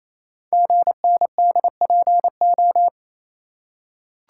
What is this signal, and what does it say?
Morse code, heard as a keyed tone, about 29 wpm, 700 Hz. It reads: GNDPO
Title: Code Groups - 5 Letters - 28wpm